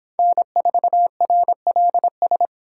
Morse code, heard as a keyed tone, about 26 wpm, 700 Hz. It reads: N4RLS